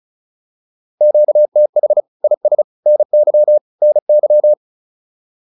Morse code, heard as a keyed tone, about 34 wpm, 600 Hz. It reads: QTH IS NY NY
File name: QSO106 35